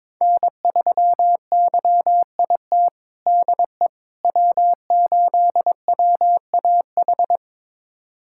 Morse code, heard as a keyed tone, about 22 wpm, 700 Hz. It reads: N3YIT DE W8WAH
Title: USA Callsign DE Callsign - 5 Characters - 22wpm